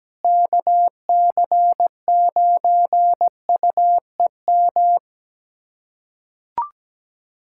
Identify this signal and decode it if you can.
Morse code, heard as a keyed tone, about 17 wpm, 700 Hz. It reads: KC9UEM
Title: USA Callsign DE Callsign - 6 Characters - 17wpm